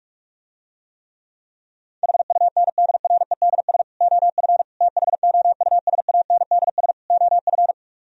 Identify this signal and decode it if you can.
Morse code, heard as a keyed tone, about 44 wpm, 700 Hz. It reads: HUNDREDS OF THOUSANDS OF